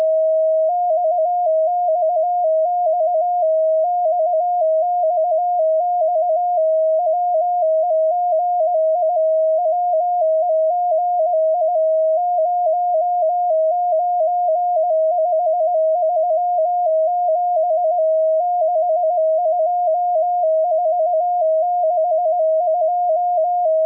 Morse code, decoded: XXX XXX WEGI WEGI 09537 62462